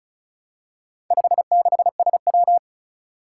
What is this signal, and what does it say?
Morse code, heard as a keyed tone, about 36 wpm, 700 Hz. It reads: 56SW
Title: Four Challenging Letter-Number Combinations 35wpm